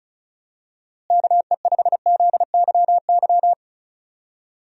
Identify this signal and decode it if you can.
Morse code, heard as a keyed tone, about 36 wpm, 700 Hz. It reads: KE5ZYY
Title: USA Call Signs - 6 Characters - Repeated 2x - 35wpm